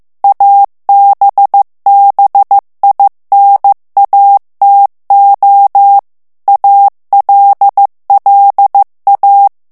Morse code, heard as a keyed tone, about 15 wpm, 800 Hz. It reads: ABBINATO ALLA
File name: ITALIEN texte clair 1 vitesse 14 mots minute